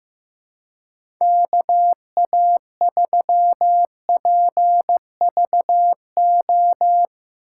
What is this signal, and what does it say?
Morse code, heard as a keyed tone, about 15 wpm, 700 Hz. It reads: KA3PVO
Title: USA Call Signs - 2x Character Spacing - 15wpm